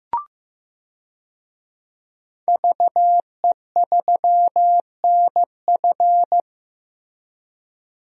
Morse code, heard as a keyed tone, about 16 wpm, 700 Hz. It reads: VE3NF
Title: DXCC-100 Call Signs - 5 Characters - Repeated 2x - 15wpm